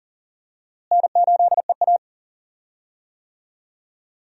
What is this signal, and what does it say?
Morse code, heard as a keyed tone, about 40 wpm, 700 Hz. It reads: N8EA
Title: USA Call Signs - 4 Characters - Repeated 3x - 40wpm